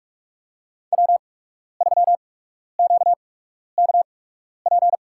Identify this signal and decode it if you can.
Morse code, heard as a keyed tone, about 46 wpm, spaced as for 12 wpm, 700 Hz. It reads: W3QKP
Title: USA Call Signs - 8x Character Spacing - 45wpm